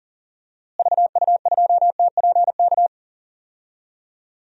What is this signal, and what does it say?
Morse code, heard as a keyed tone, about 40 wpm, 700 Hz. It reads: VU2TPK
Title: DXCC-100 Call Signs - 6 Characters - Repeated 3x - 40wpm